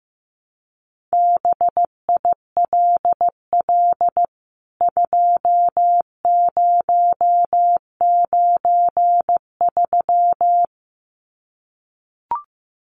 Morse code, heard as a keyed tone, about 15 wpm, 700 Hz. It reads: BILL 2093
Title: CWT Exchange 15wpm